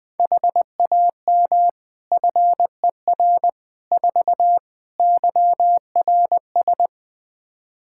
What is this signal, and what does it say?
Morse code, heard as a keyed tone, about 21 wpm, 700 Hz. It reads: HAM FER 4 YRS
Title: Practice QSO - Exchange 3 - More Info - 20wpm